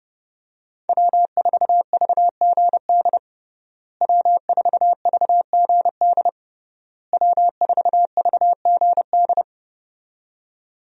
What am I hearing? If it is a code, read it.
Morse code, heard as a keyed tone, about 30 wpm, 700 Hz. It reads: W4VGD W4VGD W4VGD
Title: USA Call Signs - 5 Characters - Repeated 3x - 30wpm